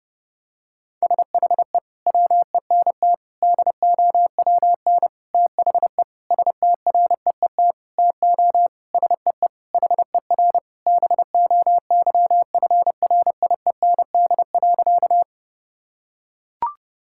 Morse code, heard as a keyed tone, about 30 wpm, 700 Hz. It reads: SHE WENT DOWN THE STREET TO SEE HER BOYFRIEND.
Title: Sentences from Top 1000 Words - Review Part 5 - 30wpm